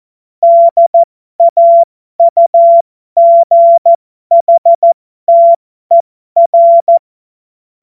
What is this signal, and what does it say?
Morse code, heard as a keyed tone, about 14 wpm, spaced as for 12 wpm, 680 Hz. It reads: DAUGHTER
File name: Morsekurs / Witze